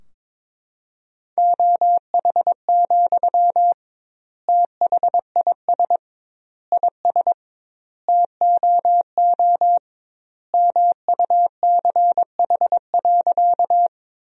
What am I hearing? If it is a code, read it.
Morse code, heard as a keyed tone, about 22 wpm, 700 Hz. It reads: OH, THIS IS TOO MUCH.
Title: 2x Word-Spacing - Sentences from Top 500 Words - 22wpm